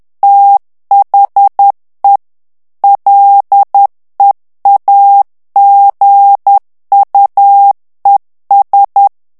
Morse code, heard as a keyed tone, about 10 wpm, 800 Hz. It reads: THE LEAGUES
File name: ANGLAIS texte clair 3 vitesse 10 mots minute